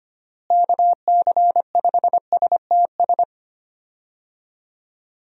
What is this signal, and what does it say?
Morse code, heard as a keyed tone, about 25 wpm, 700 Hz. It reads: KC5STS
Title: USA Call Signs - 6 Characters - Repeated 3x - 25wpm